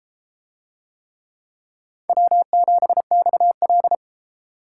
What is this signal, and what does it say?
Morse code, heard as a keyed tone, about 32 wpm, 700 Hz. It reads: W7XL